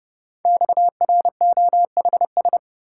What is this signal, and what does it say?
Morse code, heard as a keyed tone, about 30 wpm, 690 Hz. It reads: XROHS
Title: Code Groups - 5 Letters - 30wpm